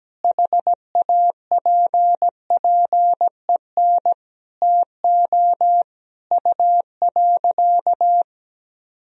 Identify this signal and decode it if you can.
Morse code, heard as a keyed tone, about 17 wpm, 700 Hz. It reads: HAPPEN TO U.
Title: Sentences from Top 500 Words and Abbreviations 17wpm